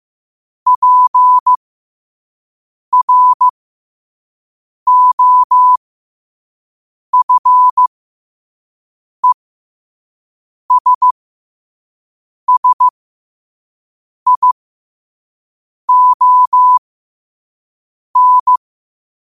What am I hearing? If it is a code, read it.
Morse code, heard as a keyed tone, about 15 wpm, spaced as for 5 wpm, 1000 Hz. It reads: PROFESSION